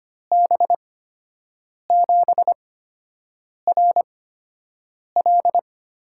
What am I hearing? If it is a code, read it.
Morse code, heard as a keyed tone, about 25 wpm, spaced as for 7 wpm, 700 Hz. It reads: B7RL
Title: USA Call Signs - 6 Characters - 8x Character Spacing - 25wpm